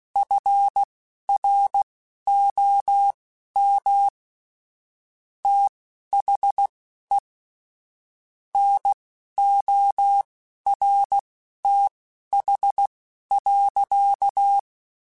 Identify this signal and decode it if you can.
Morse code, heard as a keyed tone, about 16 wpm, spaced as for 11 wpm, 800 Hz. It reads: FROM THE NORTH.
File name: Weather Sets / Weather Sets-mp3s / Weather 4 24 @10